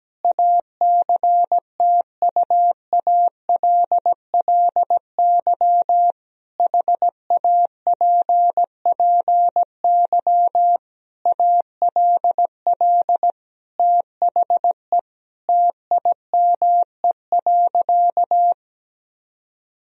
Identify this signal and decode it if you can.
Morse code, heard as a keyed tone, about 17 wpm, 700 Hz. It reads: ACTUALLY HAPPY ALL THE TIME.
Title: Sentences from Top 1000 Words - Review Part 4 - 17wpm